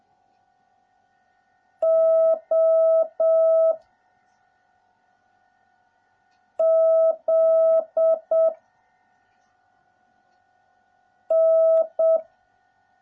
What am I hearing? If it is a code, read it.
Morse code, heard as a keyed tone, about 7 wpm, 650 Hz. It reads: O Z N